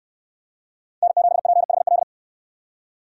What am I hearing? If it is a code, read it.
Morse code, heard as a keyed tone, about 69 wpm, 700 Hz. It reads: N7PHL